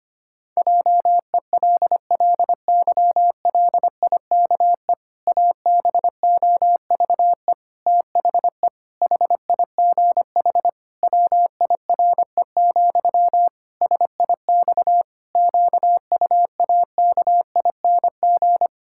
Morse code, heard as a keyed tone, about 25 wpm, 700 Hz. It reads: JELLYLIKE ABOVE THE HIGH WIRE, SIX QUAKING
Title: Pangram Challenge v2 - 25wpm